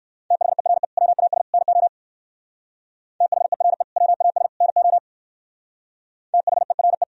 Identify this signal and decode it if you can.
Morse code, heard as a keyed tone, about 58 wpm, 700 Hz. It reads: THERE WAS NO THERE WAS NO THERE